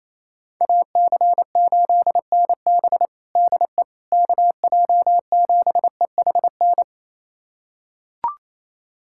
Morse code, heard as a keyed tone, about 28 wpm, 700 Hz. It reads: AC8NB DE KJ7EHN
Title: USA Callsign DE Callsign 28wpm